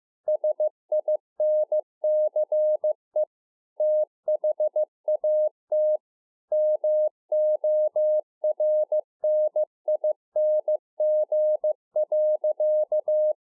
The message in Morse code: SINCE THAT MORNING.

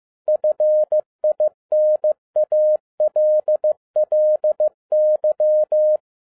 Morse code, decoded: FINALLY

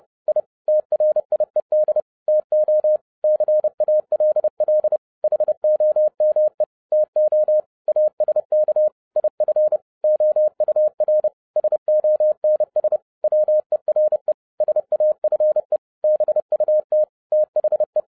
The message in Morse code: I TRIED TO CALL HOME TO ASK IF OUR SONS WERE SAFE BUT THE

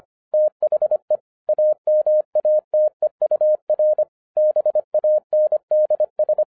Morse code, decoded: THE AMATEUR BANDS

